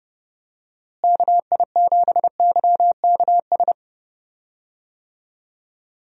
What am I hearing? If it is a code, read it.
Morse code, heard as a keyed tone, about 31 wpm, 700 Hz. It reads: KI7YKS
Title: USA Call Signs - 6 Characters - Repeated 3x - 30wpm